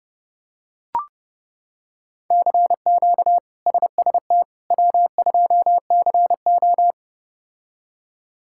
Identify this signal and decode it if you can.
Morse code, heard as a keyed tone, about 30 wpm, 700 Hz. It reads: CQ SST W2CO
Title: K1USN SST QSO Practice - 30wpm